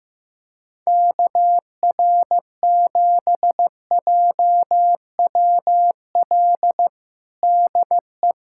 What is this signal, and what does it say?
Morse code, heard as a keyed tone, about 15 wpm, 700 Hz. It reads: KR7JWL DE
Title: USA Callsign DE Callsign - 6 Characters - 15wpm